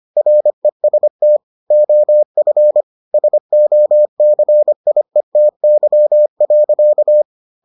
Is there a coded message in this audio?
Yes: Morse code from 0.2 to 7.2 s, heard as a keyed tone, about 25 wpm, 600 Hz. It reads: REST OF SOCIETY.